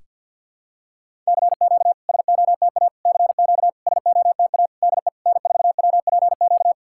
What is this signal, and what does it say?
Morse code, heard as a keyed tone, about 52 wpm, 690 Hz. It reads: CQ SOTA CQ SOTA DE N4WPQ
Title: SOTA QSO Practice 50wpm